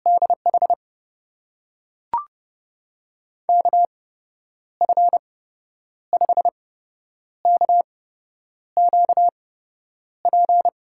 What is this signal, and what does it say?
Morse code, heard as a keyed tone, about 30 wpm, spaced as for 9 wpm, 700 Hz. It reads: DH KF5KQP
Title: USA Call Signs - 6 Characters - 8x Character Spacing - 30wpm